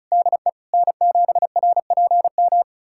Morse code, heard as a keyed tone, about 36 wpm, 700 Hz. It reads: DE N7RPM
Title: USA Callsign DE Callsign - 5 Characters - 35wpm